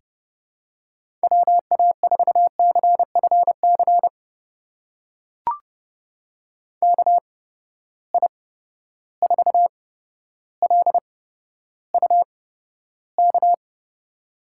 Morse code, heard as a keyed tone, about 30 wpm, spaced as for 13 wpm, 700 Hz. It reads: WA4CFC KI4LUK
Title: USA Call Signs - 6 Characters - 8x Character Spacing - 30wpm